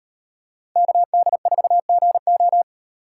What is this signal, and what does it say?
Morse code, heard as a keyed tone, about 38 wpm, 700 Hz. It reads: KD4GO